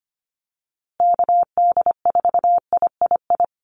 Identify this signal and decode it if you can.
Morse code, heard as a keyed tone, about 25 wpm, 700 Hz. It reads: KD4III
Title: USA Call Signs Repeated 2x 25wpm